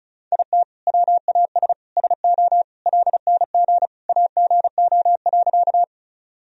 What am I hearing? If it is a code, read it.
Morse code, heard as a keyed tone, about 34 wpm, 700 Hz. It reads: IT WAS SO LNG AGO.
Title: Sentences from Top 500 Words and Abbreviations 35wpm